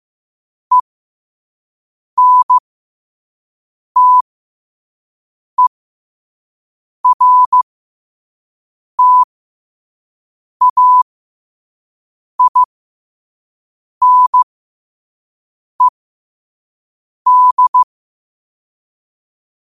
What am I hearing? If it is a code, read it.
Morse code, heard as a keyed tone, about 15 wpm, spaced as for 5 wpm, 1000 Hz. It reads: ENTERTAINED
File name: qotdfiles / 5wpm